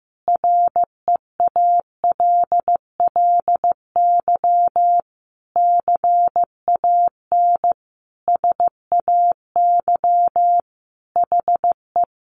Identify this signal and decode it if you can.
Morse code, heard as a keyed tone, about 15 wpm, 700 Hz. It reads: REALLY CAN SAY HE
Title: A Sound Of Thunder 15wpm